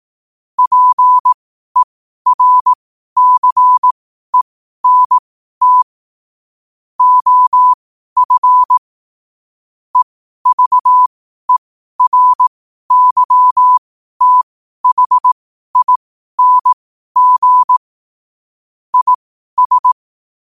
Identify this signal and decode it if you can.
Morse code, heard as a keyed tone, about 18 wpm, spaced as for 12 wpm, 1000 Hz. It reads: PERCENT OF EVERYTHING IS
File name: qotdfiles / 12wpm